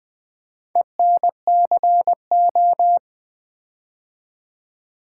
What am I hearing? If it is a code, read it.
Morse code, heard as a keyed tone, about 21 wpm, 700 Hz. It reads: ENCO
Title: Build A Word - 4 Letters At A Time - 11-Letter Words - 20wpm